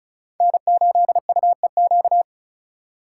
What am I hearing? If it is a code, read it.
Morse code, heard as a keyed tone, about 44 wpm, 700 Hz. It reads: N8UEQ